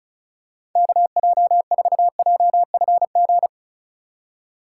Morse code, heard as a keyed tone, about 36 wpm, 700 Hz. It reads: KJ4JFG
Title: USA Call Signs - 6 Characters - 3x Character Spacing - 35wpm